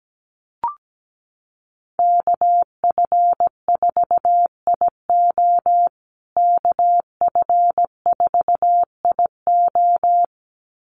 Morse code, heard as a keyed tone, about 17 wpm, 700 Hz. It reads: KF4IO KF4IO
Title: USA Call Signs Repeated 2x 17wpm